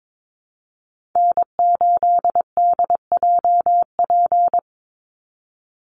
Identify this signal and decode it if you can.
Morse code, heard as a keyed tone, about 22 wpm, 700 Hz. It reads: N8DJP
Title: USA Call Signs 22wpm